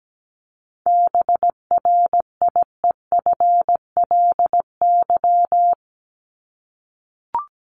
Morse code, heard as a keyed tone, about 17 wpm, 700 Hz. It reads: BRIEFLY
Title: LY Suffix - Top 100 Words with Base Words 17wpm